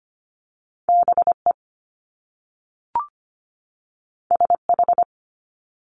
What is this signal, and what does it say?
Morse code, heard as a keyed tone, about 25 wpm, 700 Hz. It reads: BE SH
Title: Top Two Letter Combinations in Words 25wpm